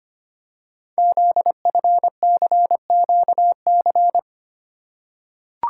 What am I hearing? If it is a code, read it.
Morse code, heard as a keyed tone, about 25 wpm, 700 Hz. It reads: ZFCQC
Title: Code Groups - 5 Letters - 25wpm